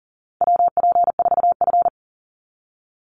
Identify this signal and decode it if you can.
Morse code, heard as a keyed tone, about 40 wpm, 700 Hz. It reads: WP4F
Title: Call Signs Repeated 2x - 40wpm